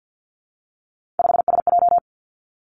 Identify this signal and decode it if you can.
Morse code, heard as a keyed tone, about 50 wpm, 700 Hz. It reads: 5SJ
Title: Three Challenging Letter-Number Combinations 50wpm